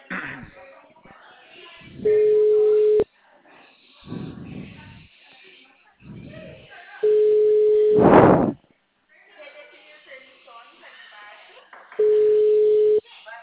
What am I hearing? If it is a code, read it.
Ringback tone (Europe, 425 Hz), with speech.